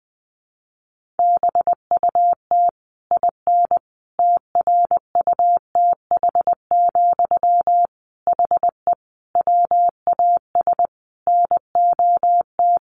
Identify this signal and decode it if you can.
Morse code, heard as a keyed tone, about 20 wpm, 700 Hz. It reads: BUT IN TRUTH, HE WAS NOT
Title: Sentences from Top 1500 Words 20wpm